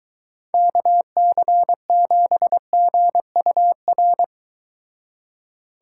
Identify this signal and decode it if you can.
Morse code, heard as a keyed tone, about 22 wpm, 700 Hz. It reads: KC7GUR